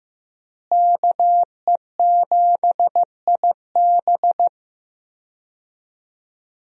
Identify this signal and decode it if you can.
Morse code, heard as a keyed tone, about 15 wpm, 700 Hz. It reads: KE7IB
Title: USA Call Signs - 5 Characters - Repeated 3x - 15wpm